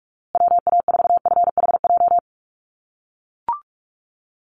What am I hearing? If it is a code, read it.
Morse code, heard as a keyed tone, about 44 wpm, 700 Hz. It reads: WA4FHJ
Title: USA Call Signs Repeated 2x 45wpm